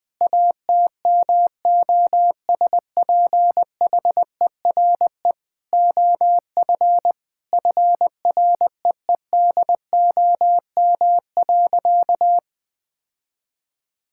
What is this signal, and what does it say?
Morse code, heard as a keyed tone, about 21 wpm, 700 Hz. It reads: ATMOSPHERE OF FREEDOM.
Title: Sentences from Top 1600 Words 20wpm